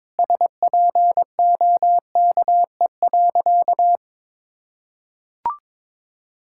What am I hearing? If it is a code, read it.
Morse code, heard as a keyed tone, about 22 wpm, 700 Hz. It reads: SPOKE.
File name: practice / 22wpm / Sentences with 3-Word Phrases - 22wpm